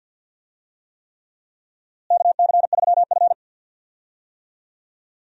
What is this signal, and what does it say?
Morse code, heard as a keyed tone, about 49 wpm, 700 Hz. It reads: KC3F
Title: USA Call Signs - 4 Characters - Repeated 3x - 50wpm